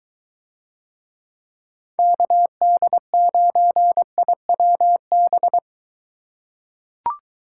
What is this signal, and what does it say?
Morse code, heard as a keyed tone, about 23 wpm, 700 Hz. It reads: KD9IWB